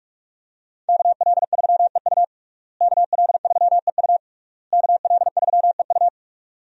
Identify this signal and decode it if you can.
Morse code, heard as a keyed tone, about 46 wpm, 700 Hz. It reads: KL3EU KL3EU KL3EU
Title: USA Call Signs - 5 Characters - Repeated 3x - 45wpm